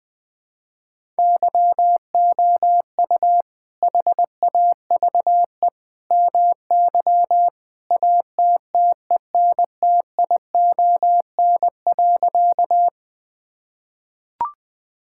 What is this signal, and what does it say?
Morse code, heard as a keyed tone, about 20 wpm, 700 Hz. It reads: YOU HAVE MY ATTENTION.
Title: Sentences from Top 1000 Words - Review Part 6 - 20wpm